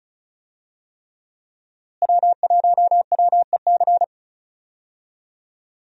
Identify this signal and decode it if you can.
Morse code, heard as a keyed tone, about 36 wpm, 700 Hz. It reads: W1WEC
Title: USA Call Signs - 5 Characters - Repeated 3x - 35wpm